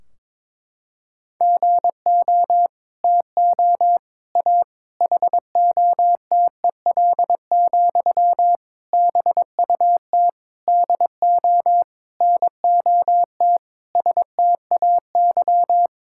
Morse code, heard as a keyed tone, about 22 wpm, 700 Hz. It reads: GO TO A HOTEL, BUT DO NOT STAY
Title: Sentences from Top 1600 Words 22wpm